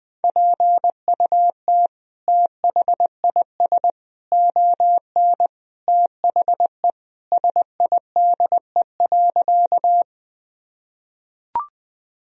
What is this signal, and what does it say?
Morse code, heard as a keyed tone, about 20 wpm, 700 Hz. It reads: PUT THIS ON THE SIDE.
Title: Sentences from Top 1000 Words - Review Part 1 - 20wpm